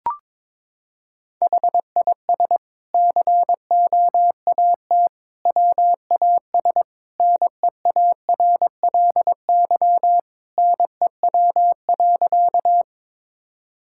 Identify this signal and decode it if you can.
Morse code, heard as a keyed tone, about 22 wpm, 700 Hz. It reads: HIS COAT WAS NEARLY NEW.
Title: Sentences from Top 1600 Words 22wpm